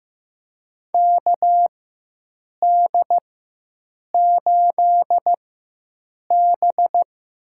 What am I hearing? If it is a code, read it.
Morse code, heard as a keyed tone, about 15 wpm, spaced as for 7 wpm, 700 Hz. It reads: KD8B
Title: USA Call Signs - 4x Character Spacing - 15wpm